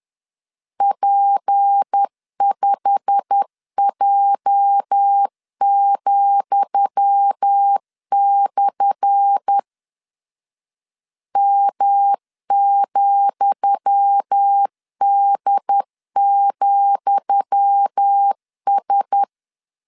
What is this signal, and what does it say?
Morse code, heard as a keyed tone, about 10 wpm, 800 Hz. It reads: P5J,/ M,D,S
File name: LEZIONI / 10 SIGNS / 10D COMA 10wpm 3013